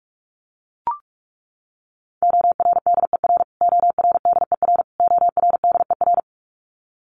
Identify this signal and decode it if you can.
Morse code, heard as a keyed tone, about 44 wpm, 700 Hz. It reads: ORDER ORDER ORDER
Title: Top 500 Words Repeated 45wpm